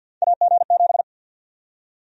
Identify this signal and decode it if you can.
Morse code, heard as a keyed tone, about 51 wpm, 700 Hz. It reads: AG7
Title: USA Call Sign Prefixes - 50wpm